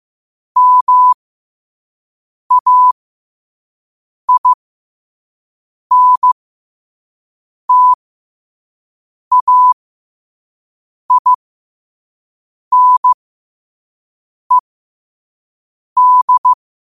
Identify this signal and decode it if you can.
Morse code, heard as a keyed tone, about 15 wpm, spaced as for 5 wpm, 1000 Hz. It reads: MAINTAINED